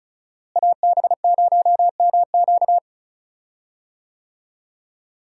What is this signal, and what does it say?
Morse code, heard as a keyed tone, about 34 wpm, 700 Hz. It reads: AB0MQ
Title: USA Call Signs - 5 Characters - Repeated 2x - 35wpm